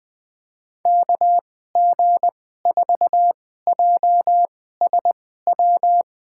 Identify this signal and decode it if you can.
Morse code, heard as a keyed tone, about 21 wpm, spaced as for 15 wpm, 700 Hz. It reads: KG4JSW
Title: USA Call Signs - 6 Characters - 2x Character Spacing - 20wpm